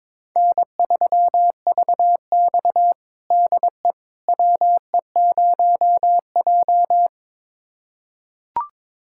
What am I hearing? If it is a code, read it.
Morse code, heard as a keyed tone, about 21 wpm, 700 Hz. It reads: N3VX DE WE0J
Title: USA Callsign DE Callsign - 4 Characters - 22wpm